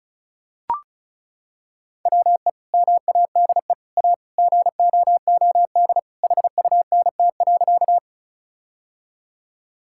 Morse code, heard as a keyed tone, about 36 wpm, 700 Hz. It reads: WE MADE A GOOD HUNT.
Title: Sentences from Top 1600 Words 35wpm